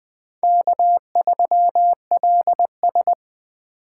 Morse code, heard as a keyed tone, about 20 wpm, 700 Hz. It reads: K3LS
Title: Speed Racing - USA Call Signs 17wpm